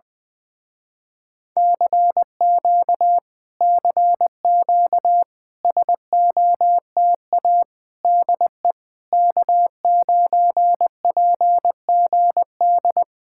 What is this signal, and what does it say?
Morse code, heard as a keyed tone, about 20 wpm, 700 Hz. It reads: CQ CQ SOTA DE K9PGD
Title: Speed-Racing Common QSO Phrases Encore - 20wpm